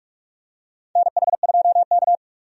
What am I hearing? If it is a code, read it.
Morse code, heard as a keyed tone, about 47 wpm, 700 Hz. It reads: NH2K
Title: USA Call Signs - 4 Characters - 45wpm